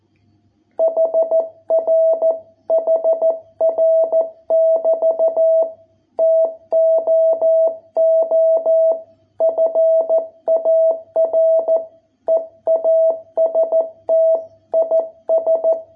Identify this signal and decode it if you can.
Morse code, heard as a keyed tone, about 14 wpm, 660 Hz. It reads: HR HR <BT> TOO FAR EAST IS